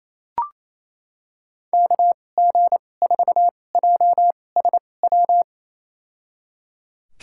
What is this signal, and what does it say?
Morse code, heard as a keyed tone, about 28 wpm, spaced as for 20 wpm, 700 Hz. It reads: KG4JSW
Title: USA Call Signs - 6 Characters - 2x Character Spacing - 28wpm